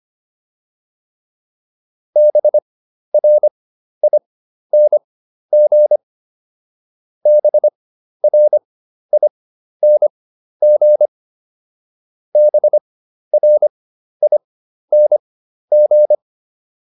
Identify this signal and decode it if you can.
Morse code, heard as a keyed tone, about 25 wpm, spaced as for 12 wpm, 600 Hz. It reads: BRING BRING BRING